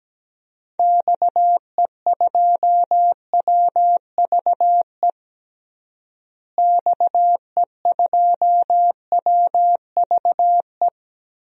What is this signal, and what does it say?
Morse code, heard as a keyed tone, about 17 wpm, 700 Hz. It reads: XE2WVE XE2WVE
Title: DXCC-100 Call Signs - 6 Characters - Repeated 2x - 17wpm